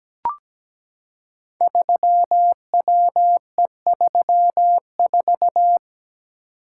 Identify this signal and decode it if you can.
Morse code, heard as a keyed tone, about 17 wpm, 700 Hz. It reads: 3WE34
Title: Code Groups - 5 Letter:Numbers - 17wpm